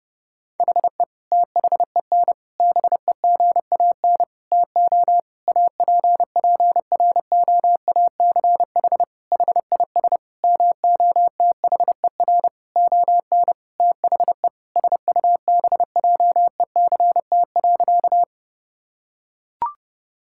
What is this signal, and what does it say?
Morse code, heard as a keyed tone, about 30 wpm, 700 Hz. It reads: HE THEN BEGAN TO APPROACH HIS MOTHER ON THE SUBJECT.
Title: Sentences from Top 1000 Words - Review Part 6 - 30wpm